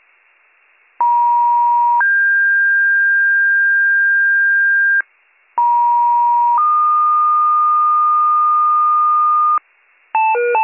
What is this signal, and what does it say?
Two-tone page, heard as a one-second tone, then a three-second tone, about 950 then 1600 Hz, about 950 then 1220 Hz, with noise. It is heard from 1.0 to 9.6 s.